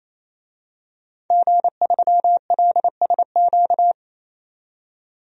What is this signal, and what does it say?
Morse code, heard as a keyed tone, about 28 wpm, 700 Hz. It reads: G3LSQ
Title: DXCC-100 Call Signs - 5 Characters - 28wpm